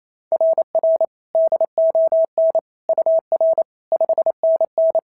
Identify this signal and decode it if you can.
Morse code, heard as a keyed tone, about 27 wpm, 690 Hz. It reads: RR DON UR 5NN
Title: SOTA QSO Practice 28wpm